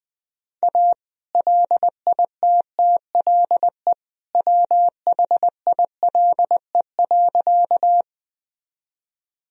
Morse code, heard as a keyed tone, about 21 wpm, 700 Hz. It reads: A LITTLE WHILE.
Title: Sentences from Top 1000 Words - Review Part 1 - 20wpm